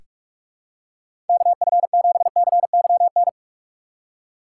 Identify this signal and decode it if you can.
Morse code, heard as a keyed tone, about 44 wpm, 700 Hz. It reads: KF7CYN
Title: USA Call Signs - 6x Character Spacing - 45wpm